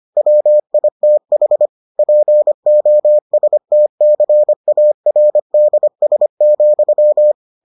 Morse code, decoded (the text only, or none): WITH POSTCARDS,